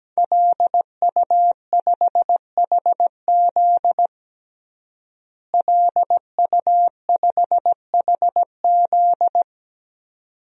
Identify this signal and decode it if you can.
Morse code, heard as a keyed tone, about 17 wpm, 700 Hz. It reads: LU5HZ LU5HZ
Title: DXCC-100 Call Signs - 5 Characters - Repeated 2x - 17wpm